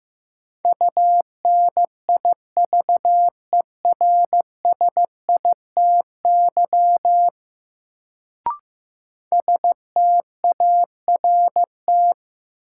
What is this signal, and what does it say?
Morse code, heard as a keyed tone, about 15 wpm, 700 Hz. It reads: UNIVERSITY START
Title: Sets of 4 Words - New Word Drill - 15wpm